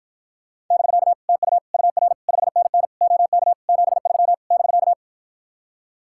Morse code, heard as a keyed tone, about 54 wpm, 700 Hz. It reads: <BK> TU UR 5NN OK 73 <BK>